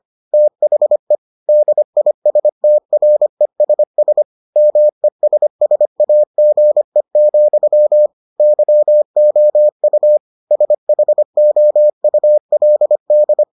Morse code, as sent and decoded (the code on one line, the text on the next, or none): - .... / -.. .. ... - .-. ... ... / -- . ... ... .- --. --..-- / -.-- --- ..- / ... .... --- ..- .-.. -..
THE DISTRESS MESSAGE, YOU SHOULD